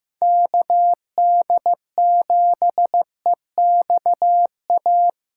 Morse code, heard as a keyed tone, about 15 wpm, 700 Hz. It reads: KD7EXA
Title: USA Callsign DE Callsign - 6 Characters - 15wpm